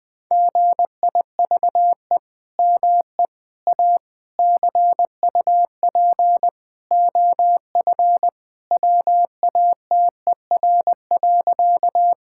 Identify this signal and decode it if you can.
Morse code, heard as a keyed tone, about 20 wpm, 700 Hz. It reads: GIVE ME A CUP OF WATER.
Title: Sentences from Top 1600 Words 20wpm